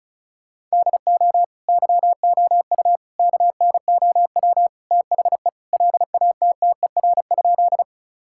Morse code, heard as a keyed tone, about 36 wpm, 700 Hz. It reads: DO YOU KNOW THE LATTER?
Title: Sentences from Top 1600 Words 35wpm